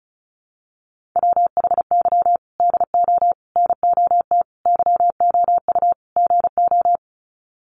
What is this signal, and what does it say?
Morse code, heard as a keyed tone, about 34 wpm, 700 Hz. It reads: WHY DO NOT YOU GO
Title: Sets of 5 Words Encore 35wpm